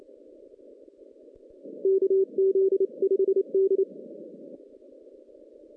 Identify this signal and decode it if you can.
Morse code, heard as a keyed tone, about 29 wpm, 390 Hz. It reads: KZ5D
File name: 2016 JAN CW NAQP / 2016 Jan naqp k5ot - 0-10